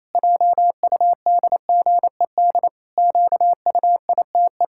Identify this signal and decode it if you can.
Morse code, heard as a keyed tone, about 28 wpm, 700 Hz. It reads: JUDGED QUITE
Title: Pangram Challenge v2 - 28wpm